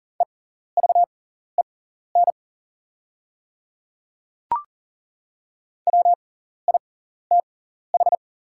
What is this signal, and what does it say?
Morse code, heard as a keyed tone, about 40 wpm, spaced as for 14 wpm, 690 Hz. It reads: EVEN WITH
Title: 6x Character Spacing - Top 200 Words - 40wpm